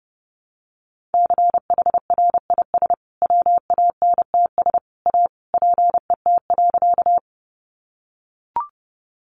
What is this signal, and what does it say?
Morse code, heard as a keyed tone, about 28 wpm, 700 Hz. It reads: CHRIS WANTS A PET.
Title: Easy Phrases 30wpm